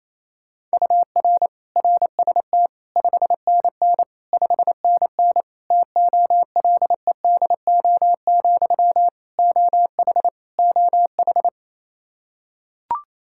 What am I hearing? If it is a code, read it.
Morse code, heard as a keyed tone, about 28 wpm, 700 Hz. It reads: UR RST 5NN 5NN TOLEDO, OH OH
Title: Practice QSO - Exchange 1 - Beginning - 28wpm